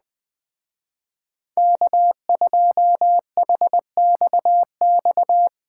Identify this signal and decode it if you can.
Morse code, heard as a keyed tone, about 19 wpm, 700 Hz. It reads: K2HXX
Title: Speed Racing - USA 1x3 Call Signs 17wpm